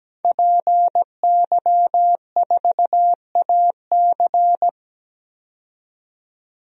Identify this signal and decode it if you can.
Morse code, heard as a keyed tone, about 17 wpm, 700 Hz. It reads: PY4AC
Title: DXCC-100 Call Signs - 5 Characters - 17wpm